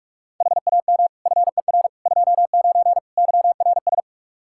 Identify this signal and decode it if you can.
Morse code, heard as a keyed tone, about 46 wpm, 700 Hz. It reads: HAM FER 29 YRS
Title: K1USN SST QSO Practice 45wpm